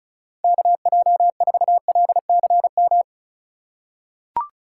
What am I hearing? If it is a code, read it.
Morse code, heard as a keyed tone, about 36 wpm, 700 Hz. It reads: KJ4LCM
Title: USA Call Signs - 6 Characters - 3x Character Spacing - 35wpm